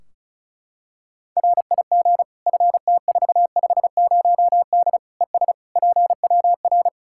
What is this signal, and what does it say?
Morse code, heard as a keyed tone, about 34 wpm, 700 Hz. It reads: RIG FT450D ES PWR
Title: Practice QSO - Exchange 2 - Info - 35wpm